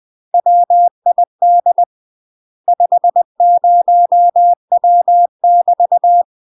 Morse code, heard as a keyed tone, about 20 wpm, 700 Hz. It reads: WID 50W <BT>
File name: UK MASTER QSOs TRAINING / 20WPM / UK MASTER QSOs 0017